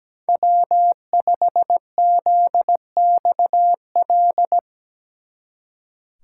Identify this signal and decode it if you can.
Morse code, heard as a keyed tone, about 20 wpm, 700 Hz. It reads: W5ZXL
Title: Speed Racing - USA 1x3 Call Signs 17wpm